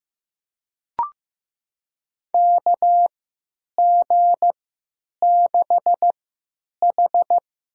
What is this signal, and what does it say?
Morse code, heard as a keyed tone, about 15 wpm, spaced as for 8 wpm, 700 Hz. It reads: KG6H
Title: USA Call Signs - 3x Character Spacing - 15wpm